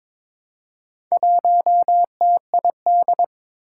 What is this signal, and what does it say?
Morse code, heard as a keyed tone, about 22 wpm, 700 Hz. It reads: 1TID
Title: Four Challenging Letter-Number Combinations 22wpm